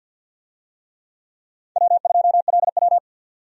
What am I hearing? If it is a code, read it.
Morse code, heard as a keyed tone, about 53 wpm, 700 Hz. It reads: W2LW
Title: USA Call Signs - 4 Characters - 50wpm